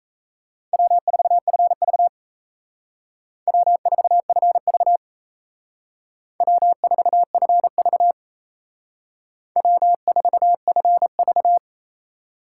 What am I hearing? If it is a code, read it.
Morse code, heard as a keyed tone, about 35 wpm, 700 Hz. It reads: W4FV W4FV W4FV W4FV
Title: Speed Racing - USA 1x2 Call Signs 28wpm